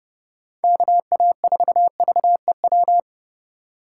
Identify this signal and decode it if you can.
Morse code, heard as a keyed tone, about 30 wpm, 700 Hz. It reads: KA4VEW
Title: USA Call Signs - 3x Character Spacing - 30wpm